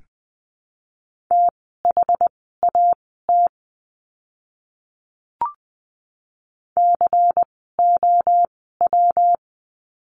Morse code, heard as a keyed tone, about 20 wpm, spaced as for 14 wpm, 700 Hz. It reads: THAT COW
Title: 115 Lesson 19f - Letter w - Words 20wpm